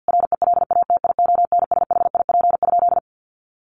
Morse code, heard as a keyed tone, about 50 wpm, 700 Hz. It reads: RELATIONSHIP?
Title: Sentences from Top 1100 Words 50wpm